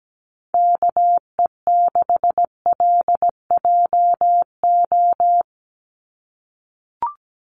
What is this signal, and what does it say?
Morse code, heard as a keyed tone, about 17 wpm, 700 Hz. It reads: KE6LJO
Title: USA Call Signs 17wpm